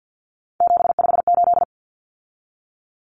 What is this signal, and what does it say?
Morse code, heard as a keyed tone, about 48 wpm, 700 Hz. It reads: 758
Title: Common Cut Numbers - 3-Digit 50wpm